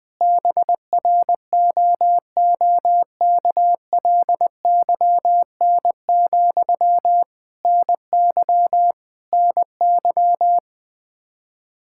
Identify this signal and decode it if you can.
Morse code, heard as a keyed tone, about 20 wpm, 700 Hz. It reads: BROOKLYN, NY NY
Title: Practice QSO - Exchange 1 - Beginning - 20wpm